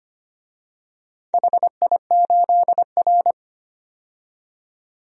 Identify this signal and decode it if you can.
Morse code, heard as a keyed tone, about 25 wpm, 700 Hz. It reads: HI8R
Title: DXCC-100 Call Signs - 4 Characters - 25wpm